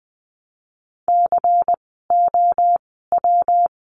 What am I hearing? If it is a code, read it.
Morse code, heard as a keyed tone, about 20 wpm, spaced as for 15 wpm, 700 Hz. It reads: COW
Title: 115 Lesson 19f - Letter w - Words 20wpm